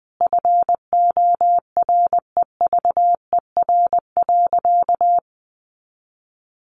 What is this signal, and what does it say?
Morse code, heard as a keyed tone, about 20 wpm, 700 Hz. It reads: FOREVER.
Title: A Sound Of Thunder 20wpm